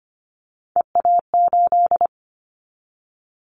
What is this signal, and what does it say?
Morse code, heard as a keyed tone, about 25 wpm, 700 Hz. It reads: EA8
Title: Callsign Prefixes for Top 100 DXCC - 25wpm